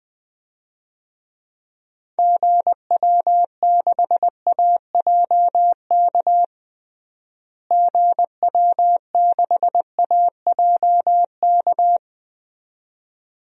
Morse code, heard as a keyed tone, about 20 wpm, 700 Hz. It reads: GW6AJK GW6AJK
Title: DXCC-100 Call Signs - 6 Characters - Repeated 2x - 20wpm